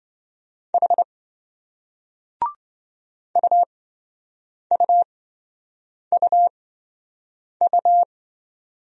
Morse code, heard as a keyed tone, about 28 wpm, spaced as for 7 wpm, 700 Hz. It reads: H UUUU